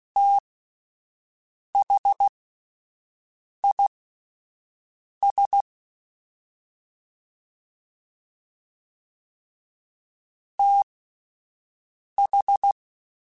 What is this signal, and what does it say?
Morse code, heard as a keyed tone, about 16 wpm, spaced as for 5 wpm, 800 Hz. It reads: THIS TH